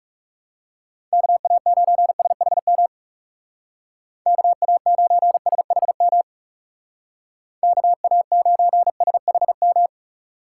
Morse code, heard as a keyed tone, about 36 wpm, 700 Hz. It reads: KA9SHM KA9SHM KA9SHM
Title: Speed Racing - USA 2x3 Call Signs 30wpm